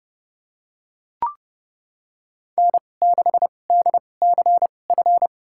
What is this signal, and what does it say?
Morse code, heard as a keyed tone, about 30 wpm, spaced as for 22 wpm, 700 Hz. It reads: N6DCF
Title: USA Call Signs - 5 Characters - 2x Character Spacing - 30wpm